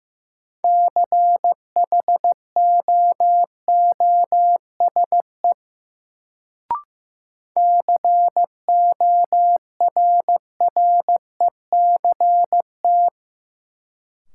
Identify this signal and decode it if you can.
Morse code, heard as a keyed tone, about 16 wpm, 700 Hz. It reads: CHOOSE CORRECT
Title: Sentences from Top 700 Words - New Word Drill - 15wpm